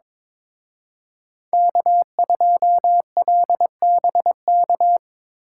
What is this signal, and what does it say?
Morse code, heard as a keyed tone, about 25 wpm, 700 Hz. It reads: K2LBK
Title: Speed Racing - USA 1x3 Call Signs 22wpm